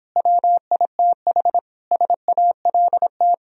Morse code, heard as a keyed tone, about 31 wpm, 700 Hz. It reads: WITH SALT